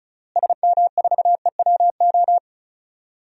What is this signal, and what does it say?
Morse code, heard as a keyed tone, about 36 wpm, 700 Hz. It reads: SM4EWO
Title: DXCC-100 Call Signs - 6 Characters - Repeated 3x - 35wpm